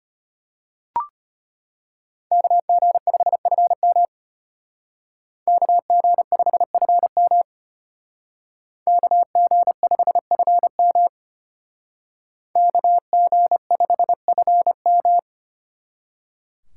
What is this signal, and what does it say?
Morse code, heard as a keyed tone, about 33 wpm, 700 Hz. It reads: KG5FM KG5FM KG5FM KG5FM